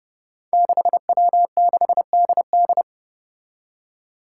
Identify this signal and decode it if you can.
Morse code, heard as a keyed tone, about 30 wpm, 700 Hz. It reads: 6W6DD
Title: Five Challenging Letter-Number Combinations 30wpm